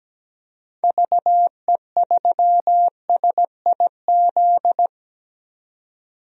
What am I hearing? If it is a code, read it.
Morse code, heard as a keyed tone, about 17 wpm, 700 Hz. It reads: VE3SIZ
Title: DXCC-100 Call Signs - 6 Characters - 17wpm